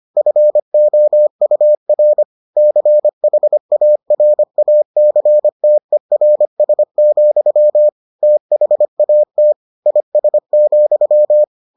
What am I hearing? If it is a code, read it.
Morse code, heard as a keyed tone, about 25 wpm, 600 Hz. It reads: FOUR CHARACTERS, THAT IS,